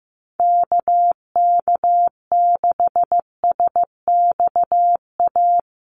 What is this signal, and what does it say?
Morse code, heard as a keyed tone, about 15 wpm, 700 Hz. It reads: KK6SXA
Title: USA Call Signs 15wpm